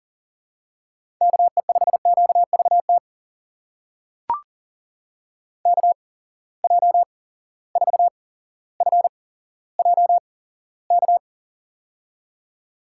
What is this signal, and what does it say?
Morse code, heard as a keyed tone, about 40 wpm, spaced as for 17 wpm, 700 Hz. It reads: KE5QVT KJ4FJK
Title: USA Call Signs - 6 Characters - 8x Character Spacing - 40wpm